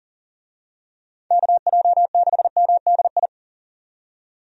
Morse code, heard as a keyed tone, about 40 wpm, 700 Hz. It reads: KJ6MDI
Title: USA Call Signs - 6 Characters - 40wpm